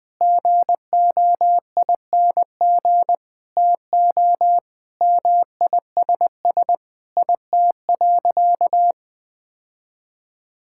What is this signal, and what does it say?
Morse code, heard as a keyed tone, about 20 wpm, 700 Hz. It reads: GOING TO MISS IT.
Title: Sentences with 7-Word Phrases - 20wpm